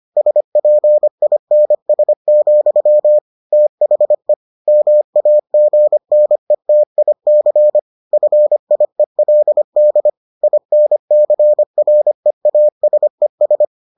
Morse code, heard as a keyed tone, about 25 wpm, 600 Hz. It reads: SPINS, THE MAGNETIC FIELD INCREASES